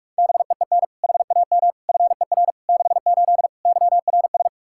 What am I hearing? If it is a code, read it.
Morse code, heard as a keyed tone, about 48 wpm, 700 Hz. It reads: BEEN HAM FER 68 YRS
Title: Practice QSO - Exchange 3 - More Info - 45wpm